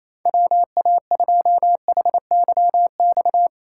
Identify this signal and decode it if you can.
Morse code, heard as a keyed tone, about 27 wpm, 700 Hz. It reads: WA2HYX
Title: USA Call Signs - 8x Character Spacing - 28wpm